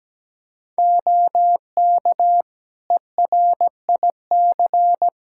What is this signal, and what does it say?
Morse code, heard as a keyed tone, about 17 wpm, 700 Hz. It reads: OK ERIC
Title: Practice QSO - Exchange 3 - More Info - 17wpm